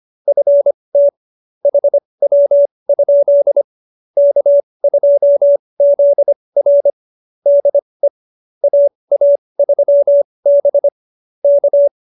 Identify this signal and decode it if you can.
Morse code, heard as a keyed tone, about 25 wpm, spaced as for 20 wpm, 570 Hz. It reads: FT HW? K2ZR DE AA3B K
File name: QSO302 20